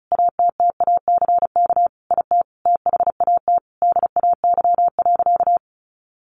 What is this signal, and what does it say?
Morse code, heard as a keyed tone, about 34 wpm, 700 Hz. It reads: ATTACK IT THAT DAY.
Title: Sentences from Top 1500 Words 35wpm